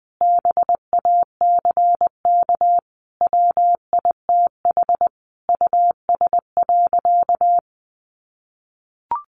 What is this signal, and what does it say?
Morse code, heard as a keyed tone, about 20 wpm, 700 Hz. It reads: BACK WITH US.
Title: A Sound Of Thunder 20wpm